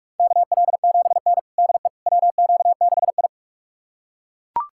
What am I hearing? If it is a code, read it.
Morse code, heard as a keyed tone, about 47 wpm, 690 Hz. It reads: KL7N DE WQ6I
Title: USA Callsign DE Callsign - 4 Characters - 45wpm